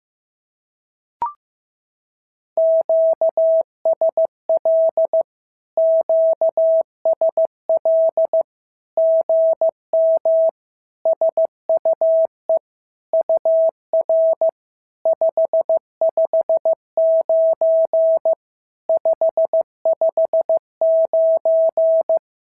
Morse code, heard as a keyed tone, about 15 wpm, 650 Hz. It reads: QSL QSL GM SUE UR 559 559
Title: SOTA QSO Practice 15wpm